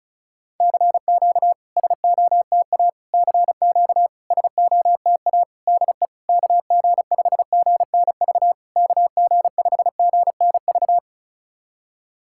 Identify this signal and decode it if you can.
Morse code, heard as a keyed tone, about 36 wpm, 690 Hz. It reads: CQ SOTA CQ SOTA DE KG5GNV KG5GNV
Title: SOTA QSO Practice 35wpm